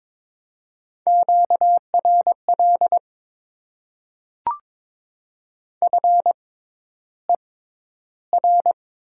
Morse code, heard as a keyed tone, about 22 wpm, spaced as for 11 wpm, 700 Hz. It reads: QRL FER
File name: Top 20 QSO Elements - 6x Character Spacing - 22wpm